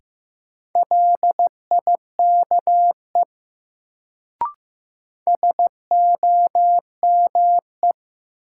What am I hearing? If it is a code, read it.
Morse code, heard as a keyed tone, about 15 wpm, 700 Hz. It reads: LIKE SOME
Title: Sentences from Top 100 Words - New Word Drill - 15wpm